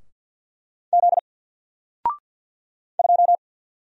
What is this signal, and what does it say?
Morse code, heard as a keyed tone, about 48 wpm, 700 Hz. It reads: Z 2
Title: Single Letter-Number Alt 50wpm